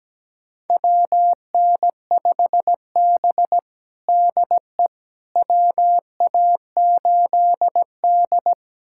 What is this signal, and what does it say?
Morse code, heard as a keyed tone, about 17 wpm, 700 Hz. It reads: WN5B DE WA8D
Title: USA Callsign DE Callsign - 4 Characters - 17wpm